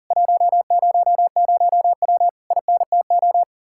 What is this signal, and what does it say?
Morse code, heard as a keyed tone, about 52 wpm, 700 Hz. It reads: 100W INTO